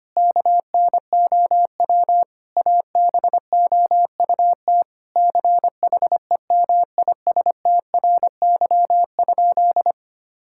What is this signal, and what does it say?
Morse code, heard as a keyed tone, about 25 wpm, 700 Hz. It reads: KNOW ABOUT CHEMISTRY?
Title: Sentences from Top 1000 Words - Review Part 6 - 25wpm